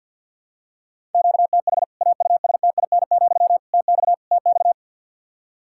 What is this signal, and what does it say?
Morse code, heard as a keyed tone, about 50 wpm, 700 Hz. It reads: QTH AUSTIN, TX TX
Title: K1USN SST QSO Practice 50wpm